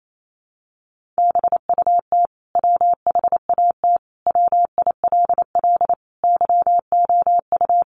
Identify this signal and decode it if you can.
Morse code, heard as a keyed tone, about 28 wpm, 700 Hz. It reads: BUT WHAT WILL YOU
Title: Sentences from Top 100 Words 28wpm